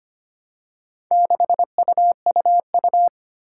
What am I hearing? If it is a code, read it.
Morse code, heard as a keyed tone, about 25 wpm, 700 Hz. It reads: 6UUU
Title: Four Challenging Letter-Number Combinations 25wpm